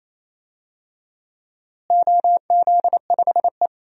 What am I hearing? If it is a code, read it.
Morse code, heard as a keyed tone, about 27 wpm, 700 Hz. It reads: OZ5E
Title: DXCC-100 Call Signs - 4 Characters - Repeated 3x - 28wpm